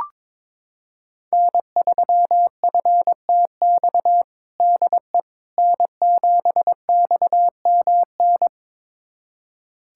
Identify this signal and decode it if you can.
Morse code, heard as a keyed tone, about 21 wpm, 700 Hz. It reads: N3FTX DE N7XMN
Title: USA Callsign DE Callsign - 5 Characters - 22wpm